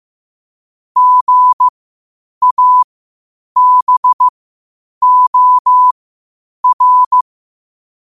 Morse code, heard as a keyed tone, about 15 wpm, spaced as for 8 wpm, 1000 Hz. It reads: GABOR
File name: qotdfiles / 8wpm